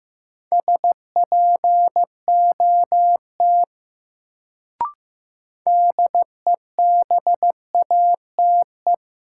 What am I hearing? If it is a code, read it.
Morse code, heard as a keyed tone, about 15 wpm, 700 Hz. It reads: SPOT DEBATE
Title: Sentences from Top 1100 Words - New Word Drill Alt - 15wpm